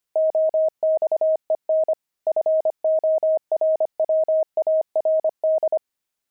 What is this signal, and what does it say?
Morse code, heard as a keyed tone, about 25 wpm, 630 Hz. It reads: OXEN FORWARD
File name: pangram-fs-25-cs-25-jaded